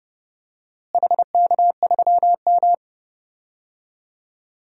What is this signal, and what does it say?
Morse code, heard as a keyed tone, about 32 wpm, 700 Hz. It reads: HK3M
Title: DXCC-100 Call Signs - 4 Characters - Repeated 2x - 30wpm